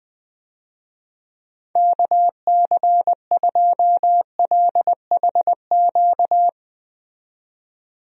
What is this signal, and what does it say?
Morse code, heard as a keyed tone, about 20 wpm, 700 Hz. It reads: KC2LHQ
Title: USA Call Signs - 6 Characters - 20wpm